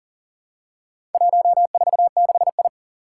Morse code, heard as a keyed tone, about 40 wpm, 700 Hz. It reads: 146I
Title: Four Challenging Letter-Number Combinations 40wpm